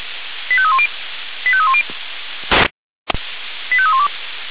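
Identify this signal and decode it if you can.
Five-tone selective call (ZVEI), heard as quick runs of five tones, tones of about 70 ms. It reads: 85310 85310 85312
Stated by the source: ZVEI five-tone selective call 85320, received with an SDR.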